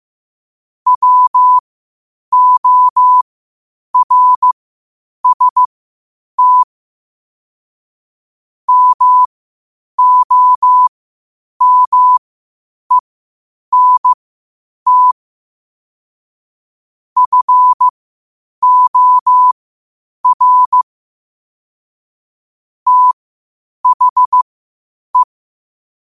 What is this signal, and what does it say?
Morse code, heard as a keyed tone, about 15 wpm, spaced as for 8 wpm, 1000 Hz. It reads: WORST MOMENT FOR THE
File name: qotdfiles / 8wpm